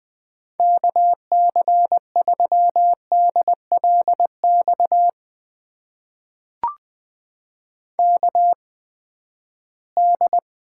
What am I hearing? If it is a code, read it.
Morse code, heard as a keyed tone, about 20 wpm, spaced as for 14 wpm, 700 Hz. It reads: KC3DLX KD
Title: USA Call Signs - 6 Characters - 8x Character Spacing - 20wpm